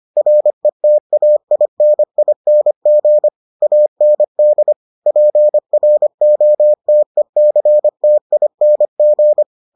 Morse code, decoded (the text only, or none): RETAINING AND PROTECTING